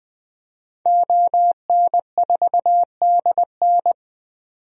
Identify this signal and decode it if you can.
Morse code, heard as a keyed tone, about 20 wpm, 700 Hz. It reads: ON4DN
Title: DXCC-100 Call Signs - 5 Characters - 20wpm